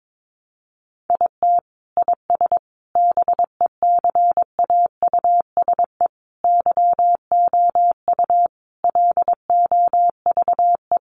Morse code, heard as a keyed tone, about 22 wpm, 700 Hz. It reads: IT IS BECAUSE YOU LOVE
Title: Sentences from Top 600 Words 22wpm